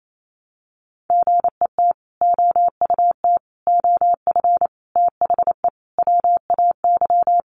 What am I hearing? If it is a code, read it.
Morse code, heard as a keyed tone, about 28 wpm, 700 Hz. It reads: GET OUT OF THE WAY
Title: Sets of 5 Words Encore 28wpm